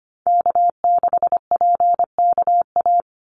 Morse code, heard as a keyed tone, about 25 wpm, 700 Hz. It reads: K6PKA
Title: USA Call Signs 25wpm